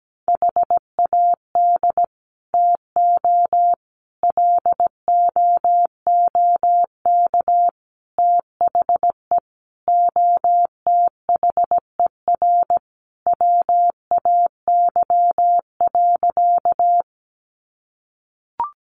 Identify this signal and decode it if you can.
Morse code, heard as a keyed tone, about 17 wpm, 700 Hz. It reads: HAD TO LOOK THE OTHER WAY.
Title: Sentences from Top 100 Words 17wpm